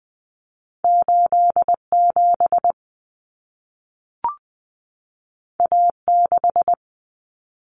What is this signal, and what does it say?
Morse code, heard as a keyed tone, about 20 wpm, 700 Hz. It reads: 87 A6
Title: Common Cut Numbers - 2-Digit 20wpm